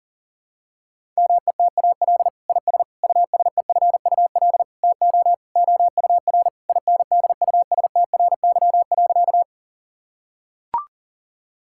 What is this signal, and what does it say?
Morse code, heard as a keyed tone, about 40 wpm, 690 Hz. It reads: METAL IS USEFUL TO OUR INDUSTRY.
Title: Sentences from Top 1000 Words - Review Part 6 - 40wpm